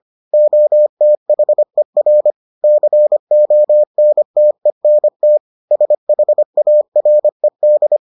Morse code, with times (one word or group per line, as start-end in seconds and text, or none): 0.3-2.3 OTHER
2.6-5.4 CONTENT
5.7-8.0 SHARED